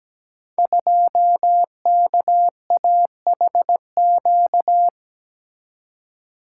Morse code, heard as a keyed tone, about 17 wpm, 700 Hz. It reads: 2KAHQ
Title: Code Groups - 5 Letter:Numbers - 17wpm